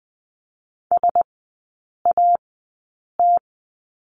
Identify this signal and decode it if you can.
Morse code, heard as a keyed tone, about 20 wpm, spaced as for 8 wpm, 700 Hz. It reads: SAT
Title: 047 Lesson 9e - Number 4 - Three Characters 20wpm